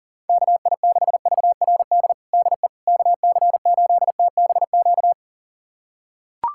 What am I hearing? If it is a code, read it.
Morse code, heard as a keyed tone, about 40 wpm, 700 Hz. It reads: KI6VRD DE KC8TBQ
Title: USA Callsign DE Callsign 40wpm